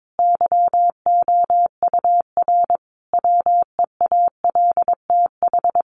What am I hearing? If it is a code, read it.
Morse code, heard as a keyed tone, about 22 wpm, 700 Hz. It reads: YOUR WEALTH
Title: Sentences from Top 1400 Words 22wpm